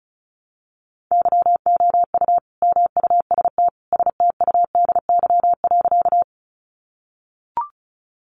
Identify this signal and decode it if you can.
Morse code, heard as a keyed tone, about 34 wpm, 700 Hz. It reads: YOU MUST STUDY.
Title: Sentences from Top 300 Words 35wpm